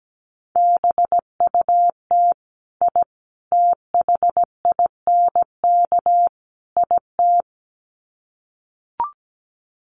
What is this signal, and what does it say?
Morse code, heard as a keyed tone, about 17 wpm, 700 Hz. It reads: BUT I THINK IT
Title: Sets of 4 Words 17wpm